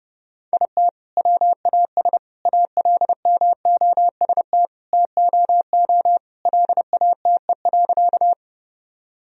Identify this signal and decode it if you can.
Morse code, heard as a keyed tone, about 32 wpm, 700 Hz. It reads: IT WAS ALMOST TOO LATE.
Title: Sentences from Top 1000 Words - Review Part 4 - 30wpm